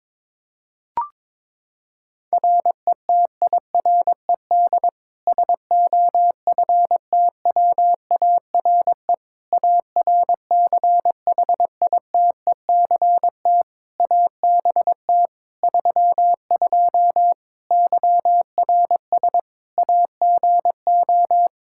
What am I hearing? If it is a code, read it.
Morse code, heard as a keyed tone, about 22 wpm, 700 Hz. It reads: RETIRED SOFTWARE ARCHITECT ABT 32 YRS AGO
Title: Practice QSO - Exchange 3 - More Info - 22wpm